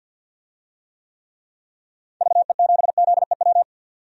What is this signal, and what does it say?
Morse code, heard as a keyed tone, about 47 wpm, 700 Hz. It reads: VE7ZEW
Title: DXCC-100 Call Signs - 6 Characters - 50wpm